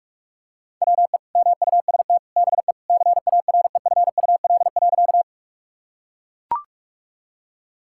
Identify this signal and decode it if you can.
Morse code, heard as a keyed tone, about 41 wpm, 700 Hz. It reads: WE MUST BE CAREFUL.
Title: Sentences from Top 1600 Words 45wpm